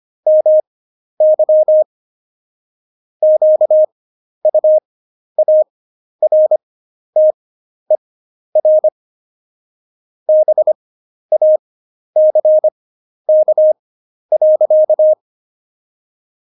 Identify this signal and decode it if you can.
Morse code, heard as a keyed tone, about 25 wpm, spaced as for 11 wpm, 620 Hz. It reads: MY QUARTER BACK.
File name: Jokes-25-char-10-eff